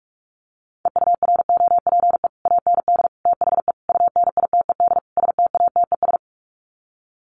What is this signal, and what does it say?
Morse code, heard as a keyed tone, about 44 wpm, 700 Hz. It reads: EUROPE AND THE UNITED STATES
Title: Sets of 5 Words Encore2 45wpm